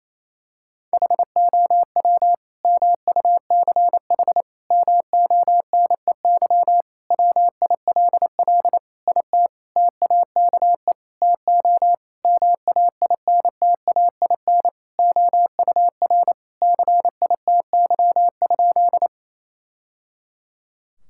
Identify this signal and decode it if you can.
Morse code, heard as a keyed tone, about 28 wpm, 700 Hz. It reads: HOW MUCH MONEY WILL IT TAKE TO MAINTAIN OUR CITY?
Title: Sentences from Top 1000 Words - Review Part 5 - 28wpm